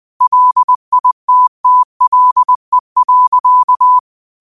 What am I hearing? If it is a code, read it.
Morse code, heard as a keyed tone, about 20 wpm, 1000 Hz. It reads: LITTLE.
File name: qotdfiles / 20wpm